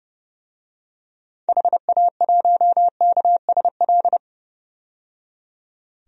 Morse code, heard as a keyed tone, about 30 wpm, 700 Hz. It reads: HA1KSL
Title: DXCC-100 Call Signs - 6 Characters - 30wpm